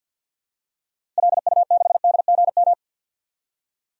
Morse code, heard as a keyed tone, about 52 wpm, 700 Hz. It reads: LU6DGM
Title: DXCC-100 Call Signs - 6 Characters - 50wpm